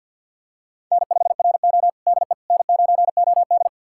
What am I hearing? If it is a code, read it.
Morse code, heard as a keyed tone, about 52 wpm, 700 Hz. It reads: N5RO DE N9OD
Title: USA Callsign DE Callsign - 4 Characters - 50wpm